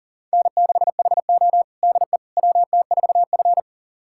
Morse code, heard as a keyed tone, about 40 wpm, 700 Hz. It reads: N6HO DE WT4F
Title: USA Callsign DE Callsign - 4 Characters - 40wpm